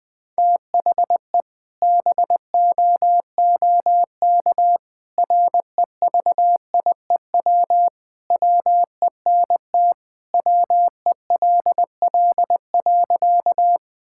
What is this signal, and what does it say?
Morse code, heard as a keyed tone, about 20 wpm, 700 Hz. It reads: THE BOOK REVIEW WENT WELL.
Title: Sentences from Top 1000 Words - Review Part 6 - 20wpm